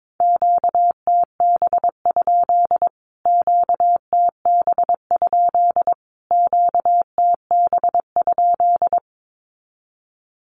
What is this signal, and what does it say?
Morse code, heard as a keyed tone, about 22 wpm, 700 Hz. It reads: QTB? QTB? QTB?
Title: Q-Codes Repeated 3x 22wpm